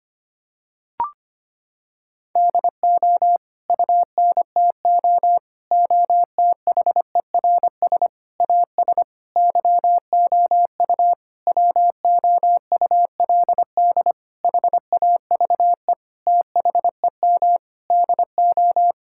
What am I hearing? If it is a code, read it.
Morse code, heard as a keyed tone, about 25 wpm, 700 Hz. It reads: DO UNTO OTHERS AS YOU WOULD HAVE THEM DO
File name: Sets of 9 Words - 25wpm